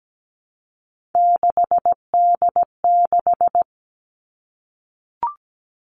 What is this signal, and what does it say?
Morse code, heard as a keyed tone, about 17 wpm, 700 Hz. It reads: 6D6
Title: Three Challenging Letter-Number Combinations 17wpm